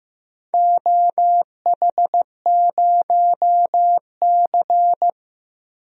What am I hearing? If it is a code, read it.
Morse code, heard as a keyed tone, about 15 wpm, 700 Hz. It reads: OH0C
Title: DXCC-100 Call Signs - 4 Characters - 15wpm